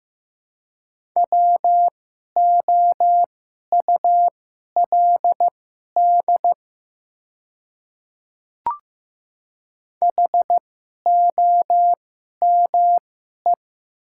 Morse code, heard as a keyed tone, about 15 wpm, spaced as for 11 wpm, 700 Hz. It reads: WOULD HOME
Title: 2x Character Spacing - Top 200 Words - 15wpm